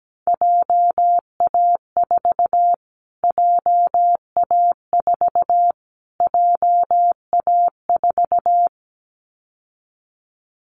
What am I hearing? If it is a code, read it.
Morse code, heard as a keyed tone, about 17 wpm, 700 Hz. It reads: JA4 JA4 JA4
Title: Callsign Prefixes for Top 25 DXCC - Repeated 3x - 17wpm